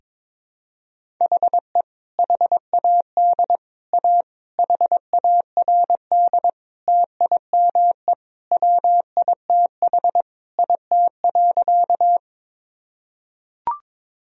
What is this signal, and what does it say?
Morse code, heard as a keyed tone, about 21 wpm, 700 Hz. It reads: HE HAD A HARD TIME WITH IT.
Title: Sentences from Top 1000 Words - Review Part 1 - 22wpm